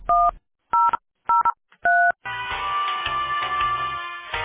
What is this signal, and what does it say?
DTMF, heard as telephone touch tones, tones of about 240 ms. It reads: 1003